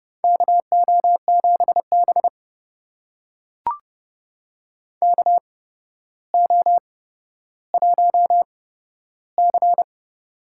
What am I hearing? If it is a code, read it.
Morse code, heard as a keyed tone, about 29 wpm, spaced as for 13 wpm, 700 Hz. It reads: KO7B KO1C
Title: USA Call Signs - 4 Characters - 8x Character Spacing - 30wpm